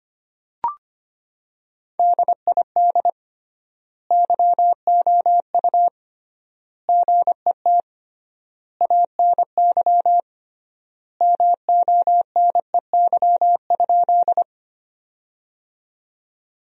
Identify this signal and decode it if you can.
Morse code, heard as a keyed tone, about 25 wpm, 700 Hz. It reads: DID YOU GET ANY MONEY?
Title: 3x Word-Spacing - Sentences from Top 400 Words - 25wpm